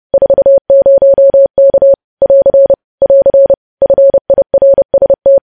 Morse code, heard as a keyed tone, about 30 wpm, 550 Hz. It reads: 40K <AR> <AR> FIRST
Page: news headlines in morse code at 30 wpm